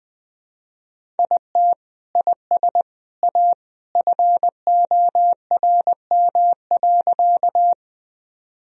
Morse code, heard as a keyed tone, about 20 wpm, 700 Hz. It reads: IT IS A FORM.
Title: Sentences from Top 1000 Words - Review Part 1 - 20wpm